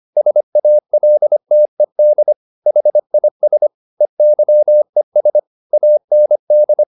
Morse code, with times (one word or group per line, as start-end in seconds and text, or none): 0.2-2.3 SALTED
2.7-3.7 HIS
4.0-5.4 EYES
5.7-6.8 AND